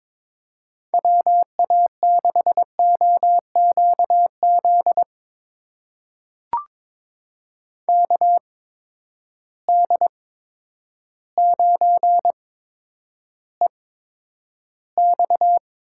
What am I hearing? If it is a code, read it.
Morse code, heard as a keyed tone, about 22 wpm, spaced as for 10 wpm, 700 Hz. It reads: WA6OQZ KD9EX
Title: USA Call Signs - 6 Characters - 8x Character Spacing - 22wpm